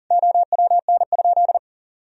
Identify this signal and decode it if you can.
Morse code, heard as a keyed tone, about 40 wpm, 700 Hz. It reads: OWN?
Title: Sentences from Top 1000 Words - Review Part 4 - 40wpm